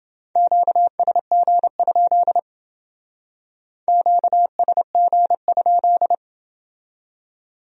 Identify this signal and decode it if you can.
Morse code, heard as a keyed tone, about 28 wpm, 700 Hz. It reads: QSG? QSG?